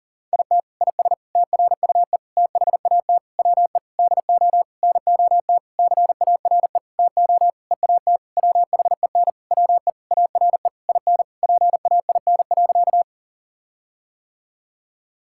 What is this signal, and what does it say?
Morse code, heard as a keyed tone, about 37 wpm, 700 Hz. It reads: IT IS TRUE THAT WE DO NOT CARE TO EAT WHEN WE ARE IN PAIN.
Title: Sentences from Top 1600 Words 40wpm